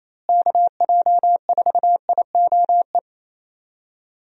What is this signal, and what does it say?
Morse code, heard as a keyed tone, about 28 wpm, 700 Hz. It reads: KJ4IOE
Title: USA Call Signs - 6 Characters - Repeated 2x - 28wpm